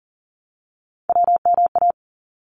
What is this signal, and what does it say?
Morse code, heard as a keyed tone, about 40 wpm, 700 Hz. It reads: WMA
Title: ARRL Sections 40wpm